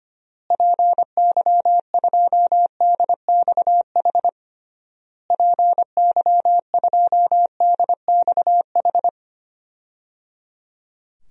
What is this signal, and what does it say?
Morse code, heard as a keyed tone, about 25 wpm, 700 Hz. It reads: PY2DXH PY2DXH
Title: DXCC-100 Call Signs - 6 Characters - Repeated 3x - 25wpm